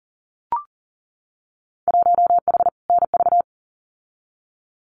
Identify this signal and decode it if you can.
Morse code, heard as a keyed tone, about 40 wpm, 700 Hz. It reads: 1H NV
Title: Winter Field Day Exchanges 40wpm